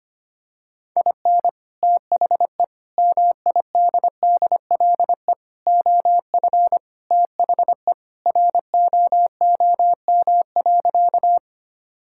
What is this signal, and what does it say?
Morse code, heard as a keyed tone, about 25 wpm, 700 Hz. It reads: IN THE MIDDLE OF THE ROOM.
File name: practice / 25wpm / Sentences with 2-Word Phrases - 25wpm